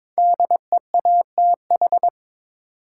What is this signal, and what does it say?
Morse code, heard as a keyed tone, about 22 wpm, 700 Hz. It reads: DEATH
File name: 5 Letter Words - First 3 Letters Given - 22wpm